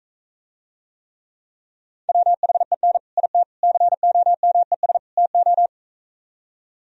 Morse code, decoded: WHEN IT COMES TO

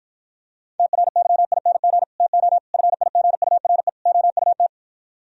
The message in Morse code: TRYING TO FIGURE OUT

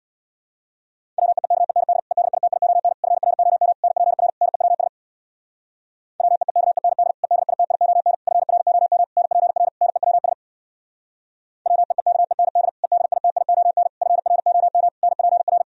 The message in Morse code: WEEKEND EDITION FROM NPR NEWS WEEKEND EDITION FROM NPR NEWS WEEKEND EDITION FROM NPR

